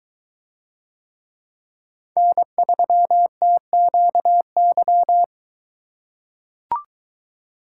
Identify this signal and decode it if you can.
Morse code, heard as a keyed tone, about 21 wpm, 700 Hz. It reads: N3TQY